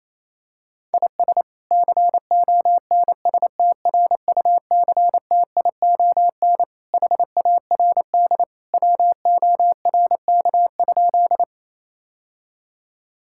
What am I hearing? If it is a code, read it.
Morse code, heard as a keyed tone, about 27 wpm, 700 Hz. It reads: IS CONSTRUCTION HARD WORK?
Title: Sentences from Top 1600 Words 28wpm